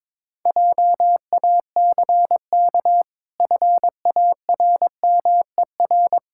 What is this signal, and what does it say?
Morse code, heard as a keyed tone, about 22 wpm, 700 Hz. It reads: JACK FARMER
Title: Pangram Challenge v2 - 22wpm